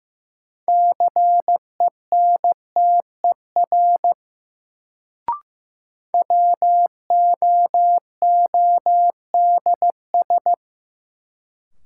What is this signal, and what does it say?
Morse code, heard as a keyed tone, about 15 wpm, 700 Hz. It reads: CENTER WOODS
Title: Sentences from Top 1000 Words - Review Part 2 - New Word Drill - 15wpm